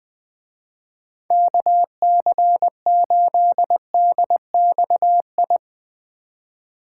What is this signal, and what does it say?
Morse code, heard as a keyed tone, about 20 wpm, 700 Hz. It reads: KC8DXI
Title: Speed Racing - USA 2x3 Call Signs 20wpm